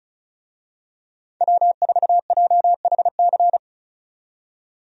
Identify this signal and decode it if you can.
Morse code, heard as a keyed tone, about 36 wpm, 700 Hz. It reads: W4JHC
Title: USA Call Signs - 5 Characters - 35wpm